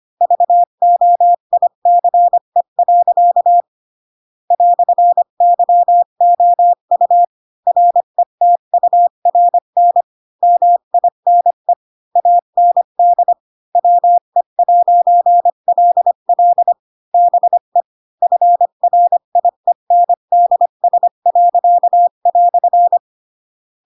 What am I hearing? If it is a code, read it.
Morse code, heard as a keyed tone, about 25 wpm, 700 Hz. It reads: VOICE. "YOU RETURN MINE AND WE'LL BE FRIENDS."